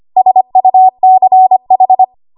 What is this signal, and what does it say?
Morse code, heard as a keyed tone, about 25 wpm, 750 Hz. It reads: SUCH
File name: Archive / 25 WPM / 25WPM